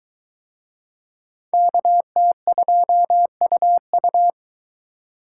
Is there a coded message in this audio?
Yes: Morse code from 1.5 to 4.3 s, heard as a keyed tone, about 21 wpm, 700 Hz. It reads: KT2UU